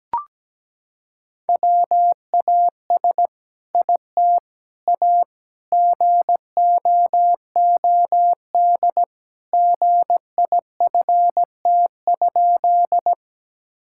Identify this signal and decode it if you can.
Morse code, heard as a keyed tone, about 17 wpm, 700 Hz. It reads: WAS IT A GOOD GIFT?
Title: Sentences from Top 1000 Words - Review Part 6 - 17wpm